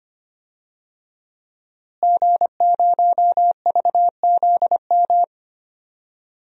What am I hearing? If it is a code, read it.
Morse code, heard as a keyed tone, about 25 wpm, 700 Hz. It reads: G0VZM
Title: DXCC-100 Call Signs - 5 Characters - Repeated 3x - 25wpm